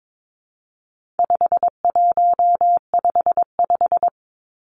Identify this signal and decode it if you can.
Morse code, heard as a keyed tone, about 22 wpm, 700 Hz. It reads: 5155
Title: Numbers - 4-Digit 22wpm